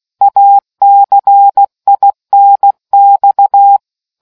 Morse code, heard as a keyed tone, about 16 wpm, 800 Hz. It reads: ACINX